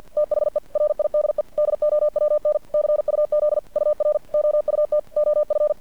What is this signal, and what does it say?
Morse code, heard as a keyed tone, about 50 wpm, 600 Hz. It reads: THE RIDE DOWN CUZ UR OUT OF
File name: archived / CRSnet